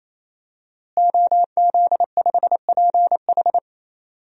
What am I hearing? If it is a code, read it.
Morse code, heard as a keyed tone, about 28 wpm, 700 Hz. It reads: OZ5PH
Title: DXCC-100 Call Signs - 5 Characters - Repeated 3x - 28wpm